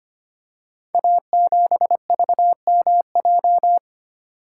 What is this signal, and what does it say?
Morse code, heard as a keyed tone, about 25 wpm, 700 Hz. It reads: A7VMJ
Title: Code Groups - 5 Letter:Numbers - 25wpm